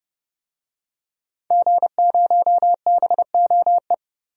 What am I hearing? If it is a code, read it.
Morse code, heard as a keyed tone, about 30 wpm, 700 Hz. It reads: G0BOE
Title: DXCC-100 Call Signs - 5 Characters - Repeated 2x - 30wpm